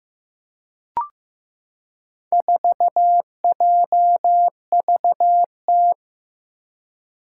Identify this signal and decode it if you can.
Morse code, heard as a keyed tone, about 15 wpm, 700 Hz. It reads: 4JVT
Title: Four Challenging Letter-Number Combinations 15wpm